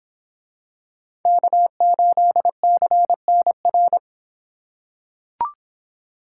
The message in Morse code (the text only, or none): K8CNR